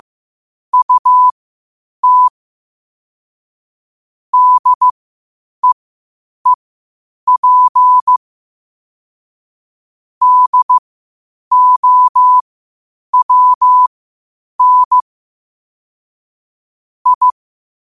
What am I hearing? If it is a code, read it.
Morse code, heard as a keyed tone, about 15 wpm, spaced as for 8 wpm, 1000 Hz. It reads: UT DEEP DOWN I
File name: qotdfiles / 8wpm